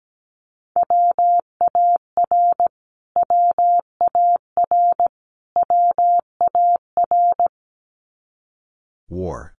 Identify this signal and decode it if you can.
Morse code, heard as a keyed tone, about 17 wpm, 700 Hz. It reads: WAR WAR WAR
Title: Top 500 Words Repeated 17wpm